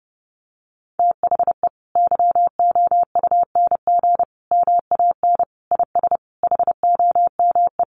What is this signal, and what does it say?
Morse code, heard as a keyed tone, about 30 wpm, 700 Hz. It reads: THE YOUNG MAN IS HOME
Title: Sentences from Top 1300 Words 30wpm